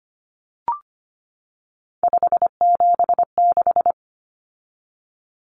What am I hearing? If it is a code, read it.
Morse code, heard as a keyed tone, about 25 wpm, 700 Hz. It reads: 576
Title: Common Cut Numbers - 3-Digit 25wpm